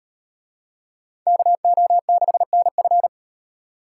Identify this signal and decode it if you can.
Morse code, heard as a keyed tone, about 35 wpm, 700 Hz. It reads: KO6NF